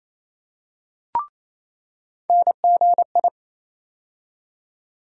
Morse code, heard as a keyed tone, about 28 wpm, 700 Hz. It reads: NGI
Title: USA Call Sign Postfixes - 3 Letters 28wpm